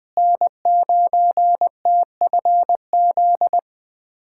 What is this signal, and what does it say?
Morse code, heard as a keyed tone, about 21 wpm, 700 Hz. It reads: N9TFZ
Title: USA Callsign DE Callsign - 5 Characters - 20wpm